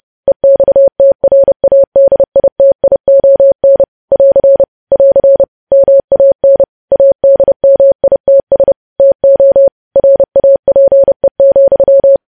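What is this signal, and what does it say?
Morse code, heard as a keyed tone, about 30 wpm, 550 Hz. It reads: EXTRADITION <AR> <AR> MAN ADMITS TO RAPE,
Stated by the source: Morse news headlines sent at 30 words per minute.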